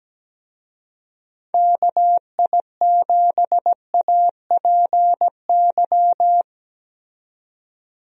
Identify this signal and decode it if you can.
Morse code, heard as a keyed tone, about 17 wpm, 700 Hz. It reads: KI7APY
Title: USA Call Signs - 6 Characters - 17wpm